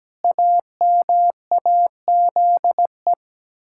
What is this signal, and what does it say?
Morse code, heard as a keyed tone, about 17 wpm, 700 Hz. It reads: AMAZE
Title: Pangram Challenge v2 - 17wpm